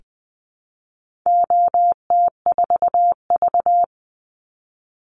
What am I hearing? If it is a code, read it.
Morse code, heard as a keyed tone, about 20 wpm, 700 Hz. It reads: OT4V
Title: Call Signs Repeated 2x - 20wpm